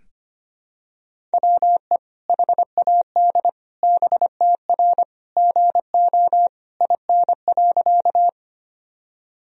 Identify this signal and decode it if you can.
Morse code, heard as a keyed tone, about 25 wpm, 700 Hz. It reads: WE HAD BTR GO IN.
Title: Sentences from Top 400 Words and Abbreviations 25wpm